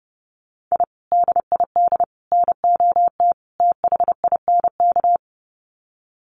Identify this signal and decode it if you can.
Morse code, heard as a keyed tone, about 30 wpm, 700 Hz. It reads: I DID NOT THINK
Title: Sets of 4 Words 30wpm